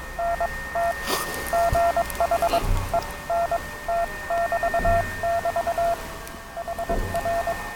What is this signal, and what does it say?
Morse code, heard as a keyed tone, about 22 wpm, spaced as for 18 wpm, 690 Hz, with noise. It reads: NT GH ENT<BT><BT> HR
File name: attachments / Gromkaja STN na TPP